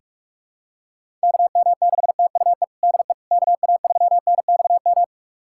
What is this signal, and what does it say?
Morse code, heard as a keyed tone, about 45 wpm, 700 Hz. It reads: KM6TUE DE KA3NXM
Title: USA Callsign DE Callsign 45wpm